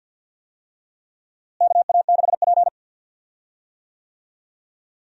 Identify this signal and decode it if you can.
Morse code, heard as a keyed tone, about 50 wpm, 700 Hz. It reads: KA6P
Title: USA Call Signs - 4 Characters - Repeated 2x - 50wpm